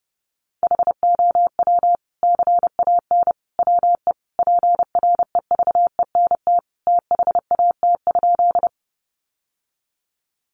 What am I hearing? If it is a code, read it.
Morse code, heard as a keyed tone, about 30 wpm, 700 Hz. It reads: HOW CAN WE PREVENT THAT?
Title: Sentences from Top 1100 Words 30wpm